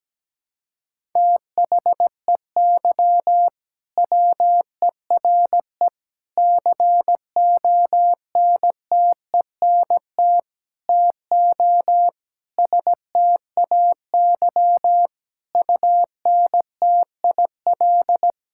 Morse code, heard as a keyed tone, about 17 wpm, 700 Hz. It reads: THEY WERE CONTENT TO STAY UNTIL
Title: Sentences from Top 1000 Words - Review Part 6 - 17wpm